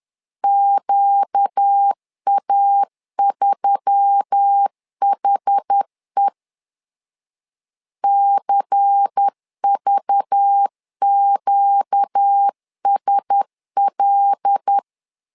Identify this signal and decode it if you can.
Morse code, heard as a keyed tone, about 10 wpm, 800 Hz. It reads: QA3HE CVQSL
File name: LEZIONI / 5 LBVC / 5 LBVC 10wpm 2349